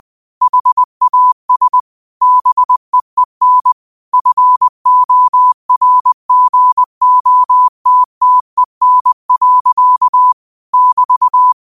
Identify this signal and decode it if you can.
Morse code, heard as a keyed tone, about 20 wpm, 1000 Hz. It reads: HAS BEEN FORGOTTEN. <BT>